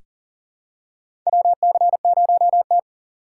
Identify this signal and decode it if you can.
Morse code, heard as a keyed tone, about 40 wpm, 700 Hz. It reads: WC0T
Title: USA Call Signs - 4 Characters - 4x Character Spacing - 40wpm